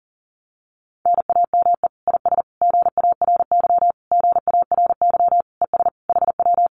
Morse code, heard as a keyed tone, about 40 wpm, 700 Hz. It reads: NAME IS GARY GARY ES HW
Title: Common QSO Phrases Encore 40wpm